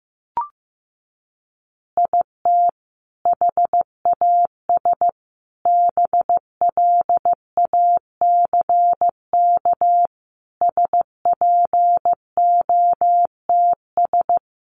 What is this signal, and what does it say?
Morse code, heard as a keyed tone, about 15 wpm, 700 Hz. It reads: IT HAS BLACK SPOTS
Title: Additional Easy Phrases 15wpm